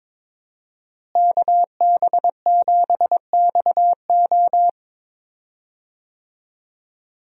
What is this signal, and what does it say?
Morse code, heard as a keyed tone, about 22 wpm, 700 Hz. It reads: KB7XO
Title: USA Call Signs - 5 Characters - Repeated 3x - 22wpm